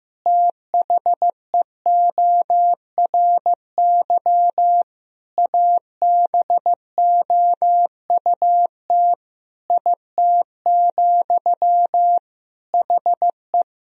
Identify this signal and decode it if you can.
Morse code, heard as a keyed tone, about 15 wpm, 700 Hz. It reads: THEORY ABOUT IT, HE
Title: Sentences from Top 1000 Words - Review Part 3 - 15wpm